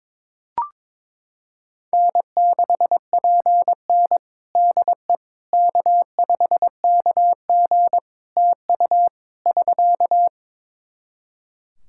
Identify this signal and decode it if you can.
Morse code, heard as a keyed tone, about 22 wpm, 700 Hz. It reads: N6PN DE K5KG TU <SK>
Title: Practice QSO - Exchange 4 - Ending - 22wpm